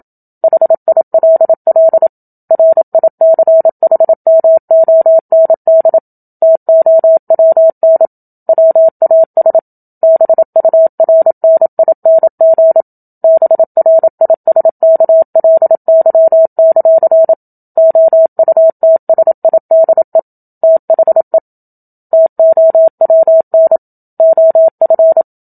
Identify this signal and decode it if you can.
Morse code, heard as a keyed tone, about 27 wpm, 650 Hz. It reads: HILL RICHMOND TOWN WAS BURNING BRISKLY; OUTSIDE THE TOWN OF
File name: Book Two / 01-Chapter 1 - 27 WPM